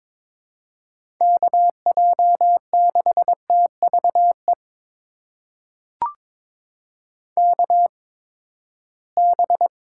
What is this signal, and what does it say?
Morse code, heard as a keyed tone, about 22 wpm, spaced as for 15 wpm, 700 Hz. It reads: KJ6TVE KB
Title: USA Call Signs - 6 Characters - 8x Character Spacing - 22wpm